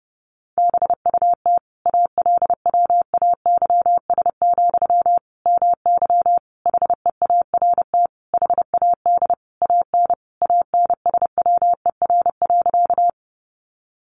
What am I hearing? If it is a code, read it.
Morse code, heard as a keyed tone, about 28 wpm, 700 Hz. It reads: BUT ALWAYS, MY HEART HAD AN ANSWER.
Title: Sentences from Top 1200 Words 30wpm